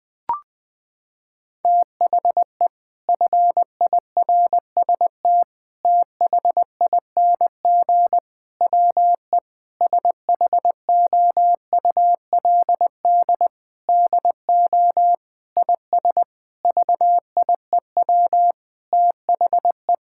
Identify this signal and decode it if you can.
Morse code, heard as a keyed tone, about 20 wpm, 700 Hz. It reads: THE FIRST THING WE SHOULD DO IS VIEW THE
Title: Sentences from Top 1000 Words - Review Part 6 - 20wpm